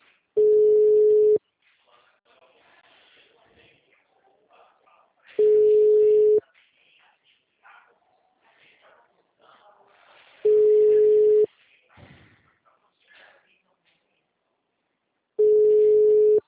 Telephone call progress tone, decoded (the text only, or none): ringback tone